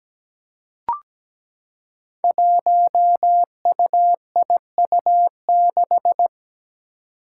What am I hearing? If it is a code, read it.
Morse code, heard as a keyed tone, about 17 wpm, 700 Hz. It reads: 1UIU6
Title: Five Challenging Letter-Number Combinations 17wpm